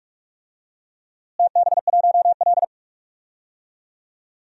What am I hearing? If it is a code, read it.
Morse code, heard as a keyed tone, about 46 wpm, 700 Hz. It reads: TB1L
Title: DXCC-100 Call Signs - 4 Characters - Repeated 2x - 45wpm